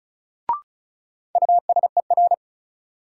Morse code, heard as a keyed tone, about 36 wpm, 700 Hz. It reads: USER
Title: Sentences from Top 1000 Words - Review Part 5 - New Word Drill - 35wpm